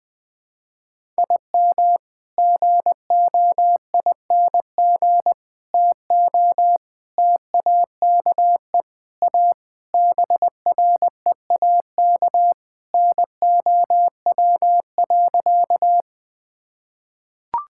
Morse code, heard as a keyed tone, about 19 wpm, 700 Hz. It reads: IM GOING TO TAKE A BREAK NOW.
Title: Sentences with 5-Word Phrases - 20wpm